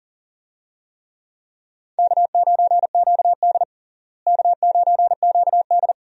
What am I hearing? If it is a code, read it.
Morse code, heard as a keyed tone, about 40 wpm, 700 Hz. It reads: K9QD K9QD
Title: USA Call Signs - 4 Characters - Repeated 2x - 40wpm